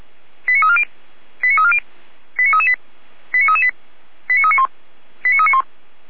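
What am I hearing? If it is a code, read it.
Five-tone selective call (ZVEI), heard as quick runs of five tones, tones of about 70 ms. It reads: 89340 89340 89338 89338 89391 89391
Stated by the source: ZVEI five-tone selective call received on SDR, code 89320.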